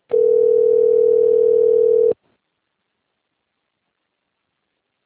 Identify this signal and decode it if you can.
Ringback tone (North America), 440 + 480 Hz.